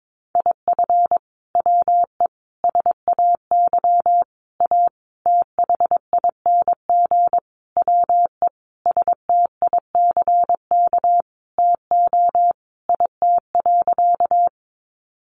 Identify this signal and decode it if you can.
Morse code, heard as a keyed tone, about 22 wpm, 700 Hz. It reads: IF WE SAY A THING WE STICK TO IT.
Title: Sentences from Top 1200 Words 22wpm